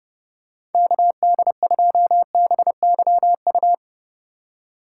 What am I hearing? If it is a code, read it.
Morse code, heard as a keyed tone, about 30 wpm, 700 Hz. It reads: KD2BYU
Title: USA Call Signs - 6 Characters - 3x Character Spacing - 30wpm